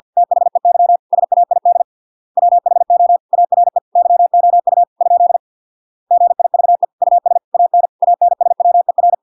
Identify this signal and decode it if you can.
Morse code, heard as a keyed tone, about 50 wpm, 700 Hz. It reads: THEY SAID WHO ARE YOU ? GIVE US AN ANSWER